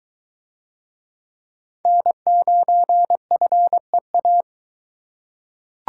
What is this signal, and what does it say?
Morse code, heard as a keyed tone, about 22 wpm, 700 Hz. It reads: N9FEA